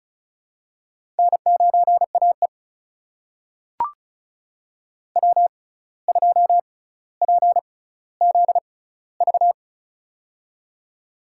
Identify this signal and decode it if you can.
Morse code, heard as a keyed tone, about 36 wpm, spaced as for 17 wpm, 700 Hz. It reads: N9AE W2PZV
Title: USA Call Signs - 6x Character Spacing - 35wpm